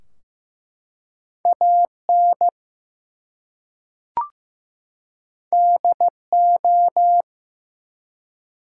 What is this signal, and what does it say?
Morse code, heard as a keyed tone, about 15 wpm, 700 Hz. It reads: AN DO
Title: Two Letter Words - 15wpm